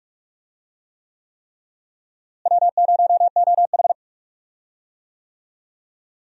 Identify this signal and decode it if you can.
Morse code, heard as a keyed tone, about 47 wpm, 700 Hz. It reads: W0OH
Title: USA Call Signs - 4 Characters - Repeated 3x - 45wpm